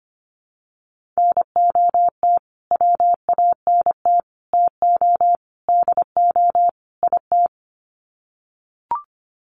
Morse code, heard as a keyed tone, about 25 wpm, 700 Hz. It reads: NOT WANT TO DO IT
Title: Sets of 5 Words Encore 25wpm